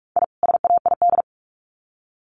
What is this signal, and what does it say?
Morse code, heard as a keyed tone, about 44 wpm, 700 Hz. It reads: I SAID
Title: Sets of 2 Words 45wpm